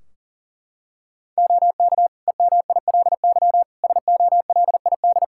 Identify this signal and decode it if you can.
Morse code, heard as a keyed tone, about 39 wpm, 700 Hz. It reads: OK EMILY SOLID
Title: Practice QSO - Exchange 3 - More Info - 40wpm